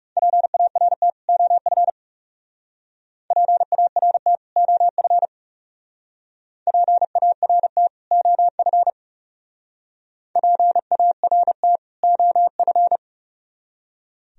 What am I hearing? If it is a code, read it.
Morse code, heard as a keyed tone, about 36 wpm, 700 Hz. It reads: PART OF PART OF PART OF PART OF
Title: Speed-Racing - Sets of 2 Words 30wpm